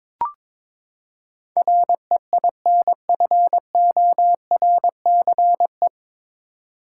Morse code, heard as a keyed tone, about 22 wpm, 700 Hz. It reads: REINFORCE
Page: Morse Code Ninja practice 22wpm